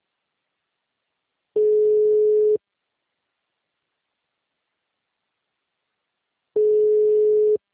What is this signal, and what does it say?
Ringback tone (Europe, 425 Hz).
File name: out- -2023- - - .56206